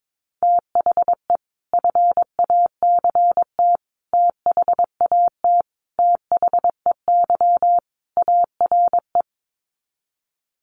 Morse code, heard as a keyed tone, about 22 wpm, 700 Hz. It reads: THE FACT THAT THEY ARE
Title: Sets of 5 Words Encore2 22wpm